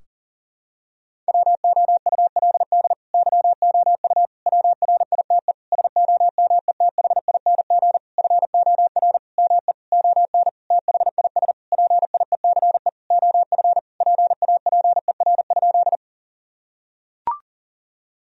Morse code, heard as a keyed tone, about 40 wpm, 700 Hz. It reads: WOULD YOU WRITE SOMETHING FOR ME ON THIS PIECE OF PAPER?
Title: Sentences from Top 1000 Words - Review Part 2 - 40wpm